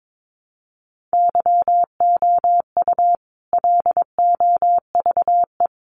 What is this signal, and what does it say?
Morse code, heard as a keyed tone, about 22 wpm, 700 Hz. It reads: YOU LOVE
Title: Sentences from Top 400 Words 22wpm